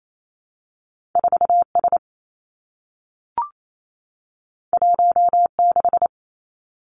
Morse code, heard as a keyed tone, about 28 wpm, 700 Hz. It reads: 4 S 1 6
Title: Two Challenging Letter-Number Combinations 28wpm